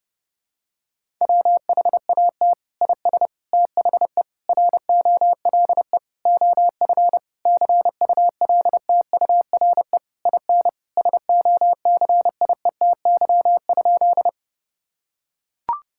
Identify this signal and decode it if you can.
Morse code, heard as a keyed tone, about 31 wpm, 700 Hz. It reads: WHAT IS THE ROLE OF CULTURE IN SOCIETY?
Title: Sentences from Top 1000 Words - Review Part 6 - 30wpm